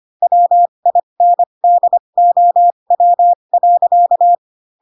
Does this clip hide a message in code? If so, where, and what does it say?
Yes: Morse code from 0.2 to 4.3 s, heard as a keyed tone, about 25 wpm, spaced as for 22 wpm, 700 Hz. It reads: WINDOW.